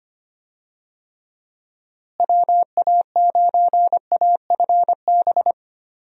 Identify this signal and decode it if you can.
Morse code, heard as a keyed tone, about 25 wpm, 700 Hz. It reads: WA9AFB
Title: USA Call Signs - 6 Characters - Repeated 3x - 25wpm